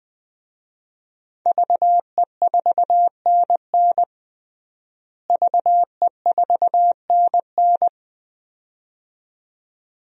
Morse code, heard as a keyed tone, about 21 wpm, 700 Hz. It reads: VE4NN VE4NN
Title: DXCC-100 Call Signs - 5 Characters - Repeated 2x - 20wpm